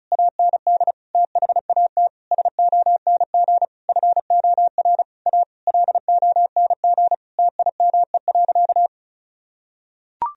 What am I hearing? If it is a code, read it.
Morse code, heard as a keyed tone, about 36 wpm, 700 Hz. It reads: AND THAT SONG FOR A LONG TIME.
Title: Sentences from Top 1000 Words - Review Part 2 - 35wpm